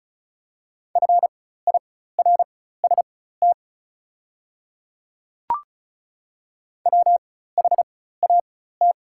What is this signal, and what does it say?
Morse code, heard as a keyed tone, about 34 wpm, spaced as for 16 wpm, 690 Hz. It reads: FIRST WHAT
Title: 4x Character Spacing - Top 100 Words - 35wpm